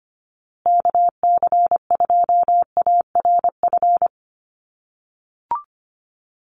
Morse code, heard as a keyed tone, about 25 wpm, 700 Hz. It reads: KC2ARF
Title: USA Call Signs Repeated 2x 25wpm